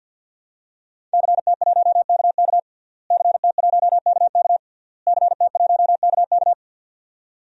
Morse code, heard as a keyed tone, about 53 wpm, 700 Hz. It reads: CT1KK CT1KK CT1KK
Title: DXCC-100 Call Signs - 5 Characters - Repeated 3x - 50wpm